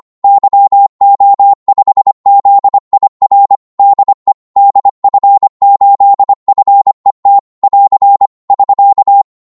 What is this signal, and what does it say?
Morse code, heard as a keyed tone, about 25 wpm, 800 Hz. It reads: YO5ZIR DE DF8FET <AR> <SK>
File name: Morse / qso-25wpm